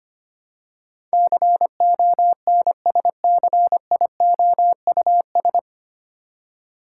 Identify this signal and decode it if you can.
Morse code, heard as a keyed tone, about 25 wpm, 700 Hz. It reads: CONSCIOUS
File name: practice / 25wpm / 9 Letter Words - First 3 Letters Given - 25wpm